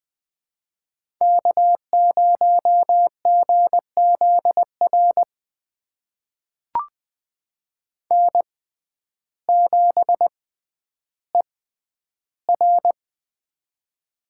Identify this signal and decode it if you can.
Morse code, heard as a keyed tone, about 20 wpm, spaced as for 11 wpm, 700 Hz. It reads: K0GZR N7ER
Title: USA Call Signs - 6x Character Spacing - 20wpm